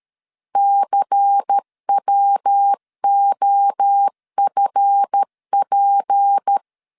Morse code, heard as a keyed tone, about 13 wpm, 800 Hz. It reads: CWOFP
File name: LEZIONI / 8 YJ45 / 8 YJ4 12wpm 2151